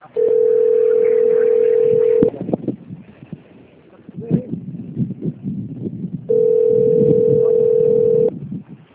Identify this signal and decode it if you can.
Ringback tone (North America), 440 + 480 Hz.